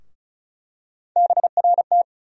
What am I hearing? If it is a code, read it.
Morse code, heard as a keyed tone, about 34 wpm, 700 Hz. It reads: BRT
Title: USA Call Sign Postfixes - 3 Letters 35wpm